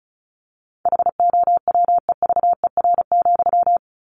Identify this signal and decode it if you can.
Morse code, heard as a keyed tone, about 34 wpm, 700 Hz. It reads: HOWEVER,
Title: Sentences from Top 600 Words 35wpm